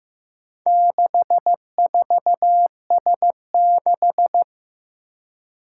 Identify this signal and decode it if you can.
Morse code, heard as a keyed tone, about 15 wpm, 700 Hz. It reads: 64S6
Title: Four Challenging Letter-Number Combinations 15wpm